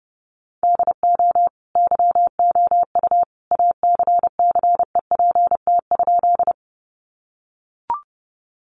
Morse code, heard as a keyed tone, about 30 wpm, 700 Hz. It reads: DO YOU ACCEPT?
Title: Sentences from Top 1300 Words 30wpm